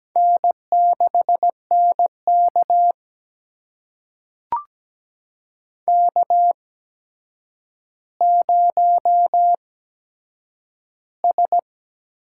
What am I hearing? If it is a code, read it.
Morse code, heard as a keyed tone, about 17 wpm, spaced as for 8 wpm, 700 Hz. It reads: N6NK K0S
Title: USA Call Signs - 4 Characters - 8x Character Spacing - 17wpm